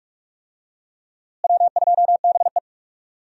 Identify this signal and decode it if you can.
Morse code, heard as a keyed tone, about 46 wpm, 700 Hz. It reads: W2BE
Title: USA Call Signs - 4 Characters - Repeated 2x - 45wpm